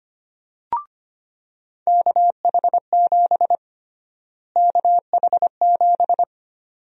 Morse code, heard as a keyed tone, about 25 wpm, 700 Hz. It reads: KH7 KH7
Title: USA Call Sign Prefixes - Repeated 2x - 25wpm